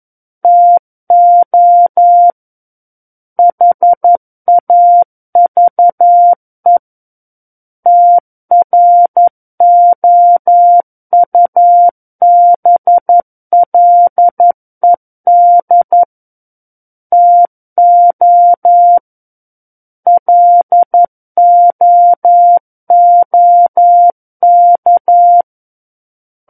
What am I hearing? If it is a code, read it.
Morse code, heard as a keyed tone, about 11 wpm, 700 Hz. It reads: TO HAVE TROUBLED TO LOOK
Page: SKCC - War of the Worlds - 02-Chapter 2 - 11 WPM